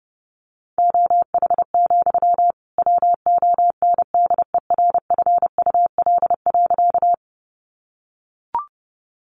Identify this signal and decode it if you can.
Morse code, heard as a keyed tone, about 30 wpm, 700 Hz. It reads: OH, WONDERFUL.
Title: Sentences from Top 1100 Words 30wpm